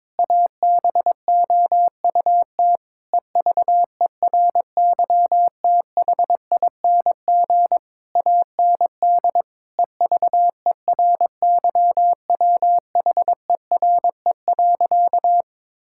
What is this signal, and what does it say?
Morse code, heard as a keyed tone, about 22 wpm, 700 Hz. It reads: ABOUT EVERYTHING AND EVERYWHERE.
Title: Sentences from Top 1600 Words 22wpm